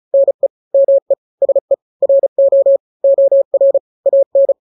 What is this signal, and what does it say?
Morse code, heard as a keyed tone, about 35 wpm, 550 Hz. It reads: NE ME SE RO OR AN
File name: n-grams / 2-grams 35 30 WPM